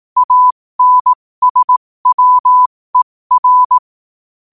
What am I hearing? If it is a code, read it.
Morse code, heard as a keyed tone, about 18 wpm, spaced as for 15 wpm, 1000 Hz. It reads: ANSWER
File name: qotdfiles / 15wpm